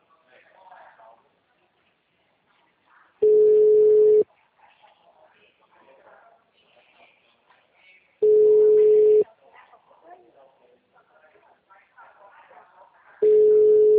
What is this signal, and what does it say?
Ringback tone (Europe, 425 Hz), with speech.